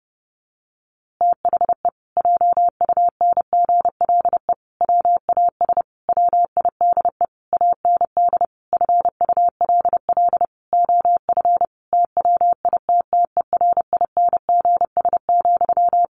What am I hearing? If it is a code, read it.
Morse code, heard as a keyed tone, about 30 wpm, 700 Hz. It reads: THE JUNGLE WAS WIDE AND FULL OF TWITTERINGS,
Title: A Sound Of Thunder 30wpm